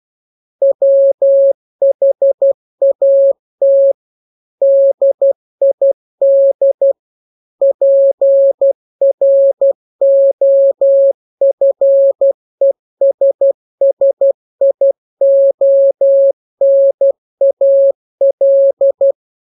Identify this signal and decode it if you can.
Morse code, heard as a keyed tone, about 12 wpm, 550 Hz. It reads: WHAT DID PROFESSIONAL
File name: jokes / joke 2 12 0000